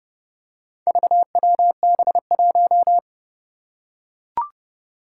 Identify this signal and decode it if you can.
Morse code, heard as a keyed tone, about 30 wpm, 700 Hz. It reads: VWB1
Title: Four Challenging Letter-Number Combinations 30wpm